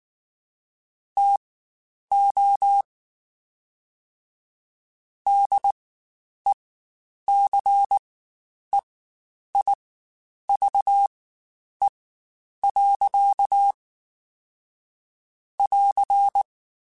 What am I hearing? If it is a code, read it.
Morse code, heard as a keyed tone, about 19 wpm, spaced as for 9 wpm, 800 Hz. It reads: TO DECEIVE. <AR>
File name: Quotes Spaced / Quotes Spaced-mp3s / Deceive sp @18